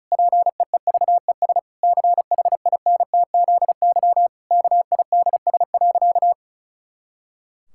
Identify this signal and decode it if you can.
Morse code, heard as a keyed tone, about 36 wpm, 700 Hz. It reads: PEEVES CHINTZY KIDS.
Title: Pangram Challenge v2 - 35wpm